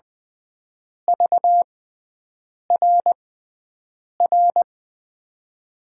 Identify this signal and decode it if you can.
Morse code, heard as a keyed tone, about 20 wpm, spaced as for 7 wpm, 700 Hz. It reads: VRR
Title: USA Call Signs - 6 Characters - 6x Character Spacing - 20wpm